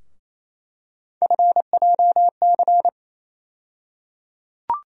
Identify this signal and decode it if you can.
Morse code, heard as a keyed tone, about 27 wpm, 700 Hz. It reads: FJC
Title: USA Call Sign Postfixes - 3 Letters 28wpm